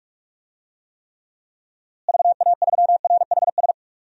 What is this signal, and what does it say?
Morse code, heard as a keyed tone, about 46 wpm, 700 Hz. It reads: VA3RHS
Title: DXCC-100 Call Signs - 6 Characters - Repeated 2x - 45wpm